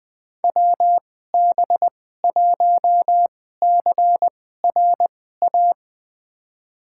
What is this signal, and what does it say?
Morse code, heard as a keyed tone, about 20 wpm, spaced as for 14 wpm, 700 Hz. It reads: WB1CRA
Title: USA Call Signs - 6 Characters - 2x Character Spacing - 20wpm